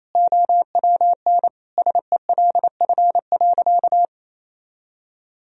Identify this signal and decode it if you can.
Morse code, heard as a keyed tone, about 27 wpm, 700 Hz. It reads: OWN SELF.
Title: Sentences from Top 1000 Words - Review Part 1 - 28wpm